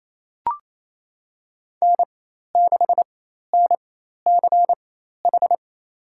Morse code, heard as a keyed tone, about 28 wpm, spaced as for 13 wpm, 700 Hz. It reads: N6NCH
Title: USA Call Signs - 4x Character Spacing - 28wpm